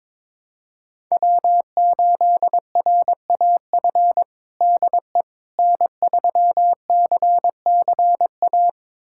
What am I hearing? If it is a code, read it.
Morse code, heard as a keyed tone, about 21 wpm, 700 Hz. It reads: W8RAF DE N3CCA
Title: USA Callsign DE Callsign - 5 Characters - 22wpm